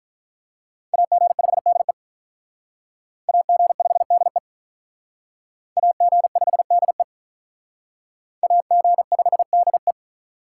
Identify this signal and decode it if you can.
Morse code, heard as a keyed tone, about 45 wpm, 700 Hz. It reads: AG5DE AG5DE AG5DE AG5DE